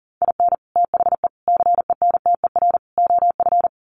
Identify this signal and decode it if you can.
Morse code, heard as a keyed tone, about 40 wpm, 700 Hz. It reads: IN THE CENTER OF
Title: Sets of 4 Words 40wpm